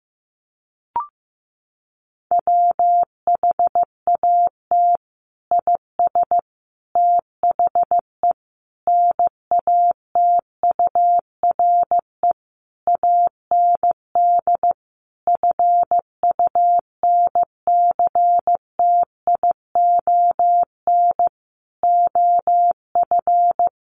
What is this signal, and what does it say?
Morse code, heard as a keyed tone, about 15 wpm, 700 Hz. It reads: WHAT IS THE NATURE AND FUNCTION OF
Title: Sentences from Top 1100 Words 15wpm